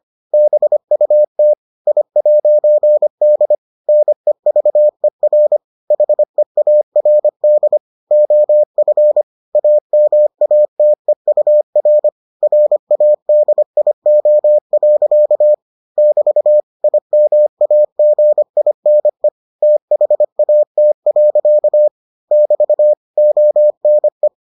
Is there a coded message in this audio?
Yes: Morse code from 0.3 to 24.3 s, heard as a keyed tone, about 25 wpm, 600 Hz. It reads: BUT I'D NEVER HEARD OF AMATEUR RADIO. <BT> IMAGINE THAT. <BT> ONE